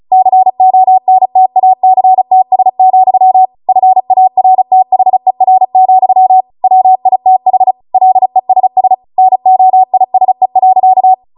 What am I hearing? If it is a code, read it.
Morse code, heard as a keyed tone, about 36 wpm, 750 Hz. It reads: CONTACTS, FARTHER, WITH LESS NOISE.